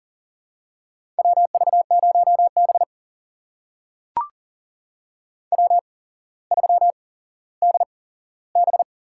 Morse code, heard as a keyed tone, about 41 wpm, spaced as for 17 wpm, 700 Hz. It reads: WV0B W3DB
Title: USA Call Signs - 4 Characters - 8x Character Spacing - 40wpm